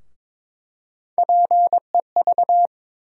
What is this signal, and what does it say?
Morse code, heard as a keyed tone, about 21 wpm, 700 Hz. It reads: PEV
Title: USA Call Sign Postfixes - 3 Letters 22wpm